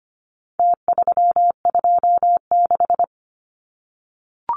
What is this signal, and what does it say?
Morse code, heard as a keyed tone, about 25 wpm, 700 Hz. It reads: T326
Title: Numbers - Contest Style - 4-Digit 25wpm